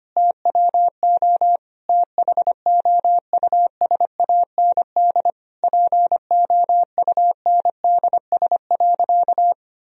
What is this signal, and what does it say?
Morse code, heard as a keyed tone, about 25 wpm, 700 Hz. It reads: TWO THOUSAND POUNDS.
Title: Sentences from Top 1000 Words - Review Part 4 - 25wpm